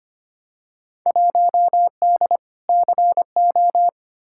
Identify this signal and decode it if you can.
Morse code, heard as a keyed tone, about 25 wpm, 700 Hz. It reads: 1D CO
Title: Field Day Exchanges - 25wpm